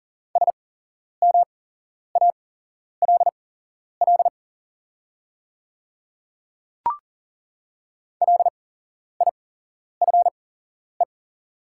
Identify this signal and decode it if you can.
Morse code, heard as a keyed tone, about 40 wpm, spaced as for 11 wpm, 690 Hz. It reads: SMALL LIFE
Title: 8x Character Spacing - Top 300 Words - 40wpm